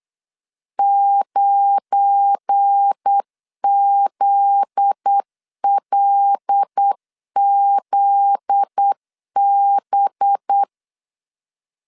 Morse code, heard as a keyed tone, about 8 wpm, 800 Hz. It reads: 9ZLZB
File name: LEZIONI / 7 GF9Z / 7 GF9Z 8wpm 3213